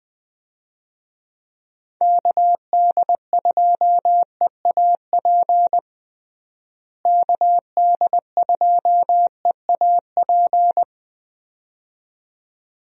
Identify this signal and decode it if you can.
Morse code, heard as a keyed tone, about 21 wpm, 700 Hz. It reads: KD2EAP KD2EAP
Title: USA Call Signs - 6 Characters - Repeated 2x - 20wpm